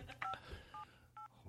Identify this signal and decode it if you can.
DTMF, heard as telephone touch tones, tones of about 120 ms. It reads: 578